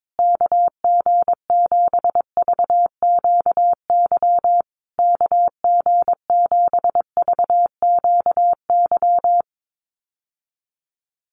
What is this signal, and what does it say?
Morse code, heard as a keyed tone, about 22 wpm, 700 Hz. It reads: KG7VQY KG7VQY
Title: USA Call Signs Repeated 3x 22wpm